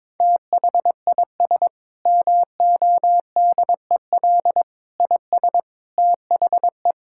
Morse code, heard as a keyed tone, about 22 wpm, 700 Hz. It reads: THIS MODEL IS THE
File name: practice / 22wpm / Sentences with 8-Word Phrases - 22wpm